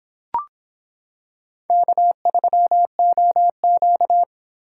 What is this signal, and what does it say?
Morse code, heard as a keyed tone, about 25 wpm, 700 Hz. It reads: K3OQ